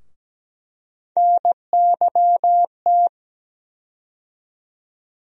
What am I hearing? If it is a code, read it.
Morse code, heard as a keyed tone, about 17 wpm, 700 Hz. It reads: NYT
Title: USA Call Sign Postfixes - 3 Letters 17wpm